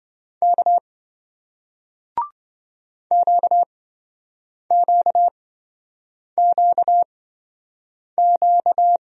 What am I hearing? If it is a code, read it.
Morse code, heard as a keyed tone, about 27 wpm, spaced as for 7 wpm, 700 Hz. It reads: K QQQQ